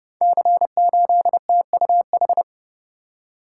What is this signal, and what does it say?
Morse code, heard as a keyed tone, about 28 wpm, 700 Hz. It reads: C8TUH
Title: Code Groups - 5 Letter:Numbers - 30wpm